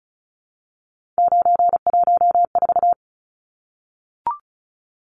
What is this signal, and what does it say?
Morse code, heard as a keyed tone, about 34 wpm, 700 Hz. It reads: 914
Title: Numbers - 3-Digit 35wpm